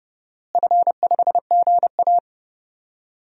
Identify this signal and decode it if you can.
Morse code, heard as a keyed tone, about 30 wpm, 700 Hz. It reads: F5GA
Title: DXCC-100 Call Signs - 4 Characters - Repeated 3x - 30wpm